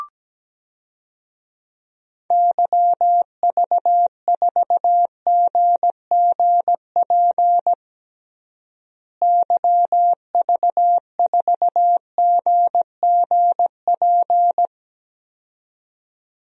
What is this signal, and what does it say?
Morse code, heard as a keyed tone, about 17 wpm, 700 Hz. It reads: YV4GGP YV4GGP
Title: DXCC-100 Call Signs - 6 Characters - Repeated 2x - 17wpm